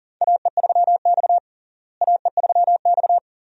Morse code, heard as a keyed tone, about 40 wpm, 700 Hz. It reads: AE3X AE3X
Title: USA Call Signs - 4 Characters - Repeated 2x - 40wpm